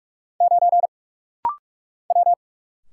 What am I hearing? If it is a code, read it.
Morse code, heard as a keyed tone, about 46 wpm, 700 Hz. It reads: 9 W
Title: Single Letter-Number - Rapid-Fire 45wpm